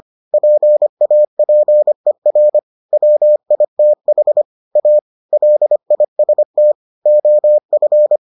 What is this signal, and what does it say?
Morse code, heard as a keyed tone, about 25 wpm, 600 Hz. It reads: PAPER WITH A LIST OF